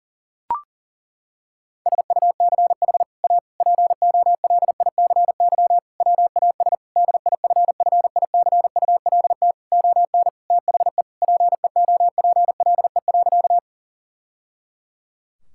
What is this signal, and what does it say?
Morse code, heard as a keyed tone, about 37 wpm, 700 Hz. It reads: SUCH A POLICY WAS DIFFICULT ON THE PEOPLE.
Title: Sentences from Top 1000 Words - Review Part 5 - 40wpm